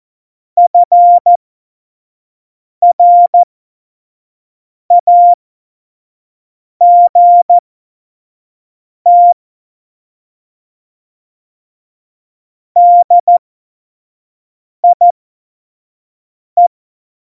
Morse code, heard as a keyed tone, about 14 wpm, spaced as for 5 wpm, 700 Hz. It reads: FRAGT DIE